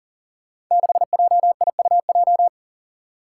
Morse code, heard as a keyed tone, about 41 wpm, 700 Hz. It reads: 6JIUJ
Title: Five Challenging Letter-Number Combinations 40wpm